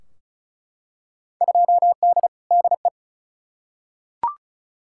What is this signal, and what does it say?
Morse code, heard as a keyed tone, about 34 wpm, 700 Hz. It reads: 2D DE
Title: Field Day Exchanges Encore - 35wpm